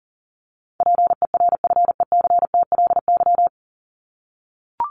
Morse code, heard as a keyed tone, about 40 wpm, 700 Hz. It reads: PERFECTLY
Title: LY Suffix - Top 100 Words with Base Words 40wpm